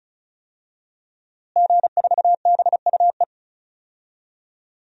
Morse code, heard as a keyed tone, about 36 wpm, 700 Hz. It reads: G4BUE
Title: DXCC-100 Call Signs - 5 Characters - Repeated 3x - 35wpm